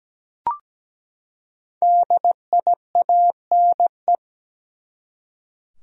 Morse code, heard as a keyed tone, about 17 wpm, 700 Hz. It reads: DIANE
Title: USA Common Names - Max 5 Letters - 17wpm